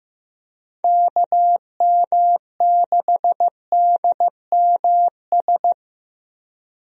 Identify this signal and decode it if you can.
Morse code, heard as a keyed tone, about 15 wpm, 700 Hz. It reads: KM6DMS
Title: USA Call Signs - 6 Characters - Repeated 3x - 15wpm